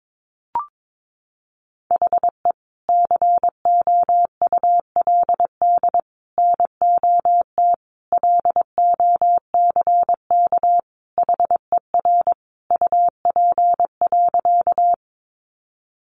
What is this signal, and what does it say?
Morse code, heard as a keyed tone, about 22 wpm, 700 Hz. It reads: HE COULD NOT LOCK HER UP.
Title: Sentences from Top 1300 Words 22wpm